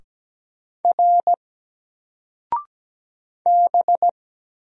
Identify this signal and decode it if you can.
Morse code, heard as a keyed tone, about 17 wpm, 700 Hz. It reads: R B
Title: Single Letter-Number Alt 17wpm